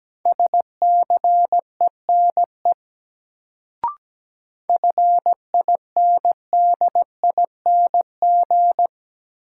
Morse code, heard as a keyed tone, about 17 wpm, 700 Hz. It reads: SCENE FINDING
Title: Sentences from Top 1100 Words - New Word Drill Alt - 17wpm